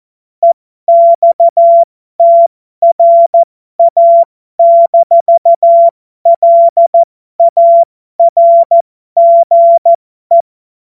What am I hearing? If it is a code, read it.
Morse code, heard as a keyed tone, about 14 wpm, spaced as for 12 wpm, 680 Hz. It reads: EXTRA-LARGE
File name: Morsekurs / Witze